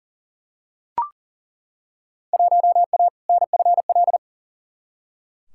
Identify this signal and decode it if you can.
Morse code, heard as a keyed tone, about 40 wpm, 700 Hz. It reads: 1A NFL
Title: Field Day Exchanges Encore - 40wpm